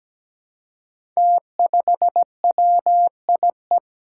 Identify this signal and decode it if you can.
Morse code, heard as a keyed tone, about 17 wpm, 700 Hz. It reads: T5WIE
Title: Five Challenging Letter-Number Combinations 17wpm